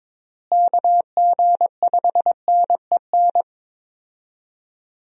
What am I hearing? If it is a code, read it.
Morse code, heard as a keyed tone, about 22 wpm, 700 Hz. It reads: KG5NEN
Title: USA Call Signs - 6 Characters - Repeated 2x - 22wpm